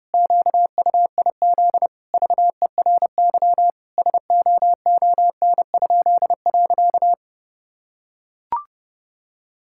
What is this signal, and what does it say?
Morse code, heard as a keyed tone, about 30 wpm, 700 Hz. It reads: QUIZ VERY SOON?.
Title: Pangram Challenge v2 - 30wpm